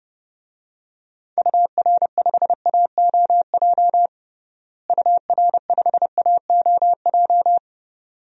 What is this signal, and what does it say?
Morse code, heard as a keyed tone, about 30 wpm, 700 Hz. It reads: UR5AOJ UR5AOJ
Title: DXCC-100 Call Signs - 6 Characters - Repeated 2x - 30wpm